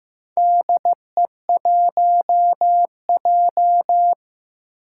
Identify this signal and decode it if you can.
Morse code, heard as a keyed tone, about 15 wpm, 700 Hz. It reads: DE1J
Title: Four Challenging Letter-Number Combinations 15wpm